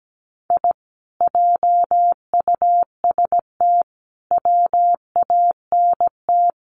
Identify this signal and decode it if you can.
Morse code, heard as a keyed tone, about 17 wpm, 700 Hz. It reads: I JUST WANT
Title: Sets of 4 Words 17wpm